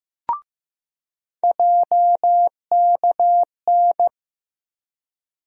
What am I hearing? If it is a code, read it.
Morse code, heard as a keyed tone, about 15 wpm, 700 Hz. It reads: JKN
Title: USA Call Sign Postfixes - 3 Letters 15wpm